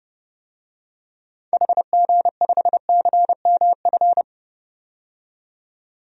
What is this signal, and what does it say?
Morse code, heard as a keyed tone, about 30 wpm, 700 Hz. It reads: HG5CMF
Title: DXCC-100 Call Signs - 6 Characters - 30wpm